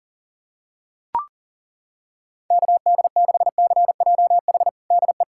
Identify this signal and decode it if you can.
Morse code, heard as a keyed tone, about 40 wpm, 700 Hz. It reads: KD6CJH DE
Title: USA Callsign DE Callsign 40wpm